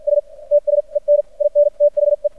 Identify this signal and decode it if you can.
Morse code, heard as a keyed tone, about 58 wpm, spaced as for 40 wpm, 590 Hz. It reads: STS EN IN THE
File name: archived / CRSnet